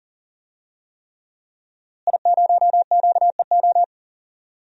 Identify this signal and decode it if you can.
Morse code, heard as a keyed tone, about 40 wpm, 700 Hz. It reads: I0QEO
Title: DXCC-100 Call Signs - 5 Characters - Repeated 3x - 40wpm